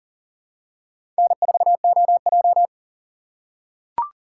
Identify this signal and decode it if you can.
Morse code, heard as a keyed tone, about 40 wpm, 700 Hz. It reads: N4OJ
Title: USA Call Signs - 4 Characters - 2x Character Spacing - 40wpm